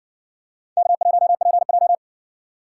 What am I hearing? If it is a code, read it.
Morse code, heard as a keyed tone, about 58 wpm, 700 Hz. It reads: K1PJ